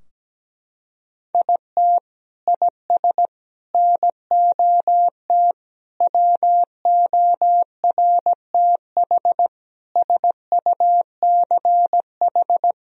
Morse code, heard as a keyed tone, about 17 wpm, 700 Hz. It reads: IT IS NOT WORTH SUCH
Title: Sentences from Top 1000 Words - Review Part 3 - 17wpm